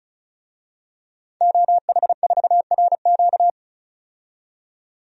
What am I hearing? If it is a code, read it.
Morse code, heard as a keyed tone, about 34 wpm, 700 Hz. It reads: OH4RQ
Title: DXCC-100 Call Signs - 5 Characters - 35wpm